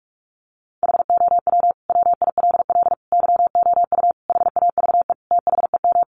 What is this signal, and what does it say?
Morse code, heard as a keyed tone, about 44 wpm, 700 Hz. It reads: HOW WILL YOU HAVE THEM
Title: Sentences from Top 900 Words 45wpm